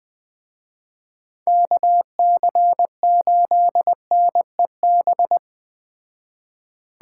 Morse code, heard as a keyed tone, about 20 wpm, 700 Hz. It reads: KC8NEB